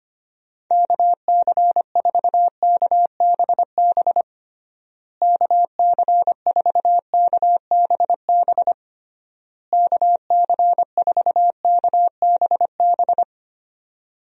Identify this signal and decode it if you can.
Morse code, heard as a keyed tone, about 25 wpm, 700 Hz. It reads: KC4KBB KC4KBB KC4KBB
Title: USA Call Signs - 6 Characters - Repeated 3x - 25wpm